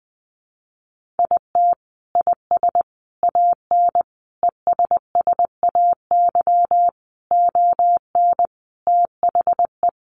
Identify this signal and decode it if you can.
Morse code, heard as a keyed tone, about 20 wpm, 700 Hz. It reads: IT IS AN ESSAY ON THE
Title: Sentences from Top 1500 Words 20wpm